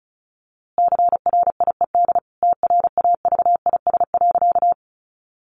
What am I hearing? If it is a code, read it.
Morse code, heard as a keyed tone, about 34 wpm, 700 Hz. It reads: CRIED TRAVIS.
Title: A Sound Of Thunder 35wpm